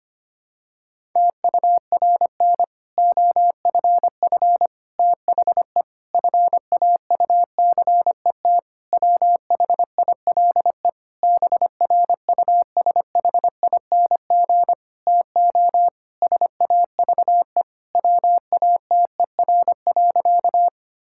Morse code, heard as a keyed tone, about 25 wpm, 700 Hz. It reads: TURN OFF THE FAUCET WHILE BRUSHING TO SAVE WATER.
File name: practice / 25wpm / Difficult Sentences with Hints - 25wpm